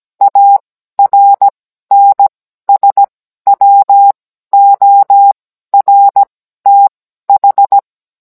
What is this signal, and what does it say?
Morse code, heard as a keyed tone, about 17 wpm, spaced as for 12 wpm, 800 Hz. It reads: ARNSWORTH